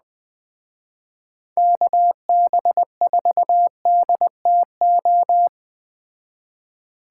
Morse code, heard as a keyed tone, about 20 wpm, 700 Hz. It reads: KB4DTO